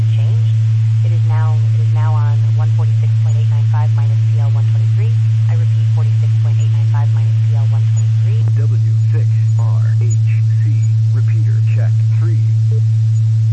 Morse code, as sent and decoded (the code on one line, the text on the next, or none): none